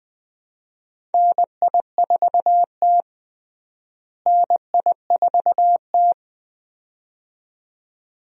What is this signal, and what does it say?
Morse code, heard as a keyed tone, about 21 wpm, 700 Hz. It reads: NI4T NI4T
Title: USA Call Signs - 4 Characters - Repeated 2x - 20wpm